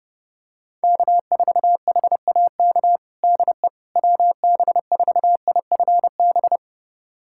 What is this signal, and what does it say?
Morse code, heard as a keyed tone, about 30 wpm, 700 Hz. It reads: K4HAK DE WB4IFB
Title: USA Callsign DE Callsign 30wpm